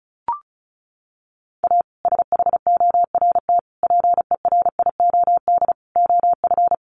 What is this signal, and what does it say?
Morse code, heard as a keyed tone, about 34 wpm, 700 Hz. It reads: A SHORT PERIOD OF
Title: Sets of 5 Words Encore 35wpm